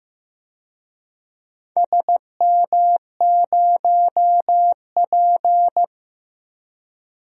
Morse code, heard as a keyed tone, about 15 wpm, 700 Hz. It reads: SM0P
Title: DXCC-100 Call Signs - 4 Characters - 15wpm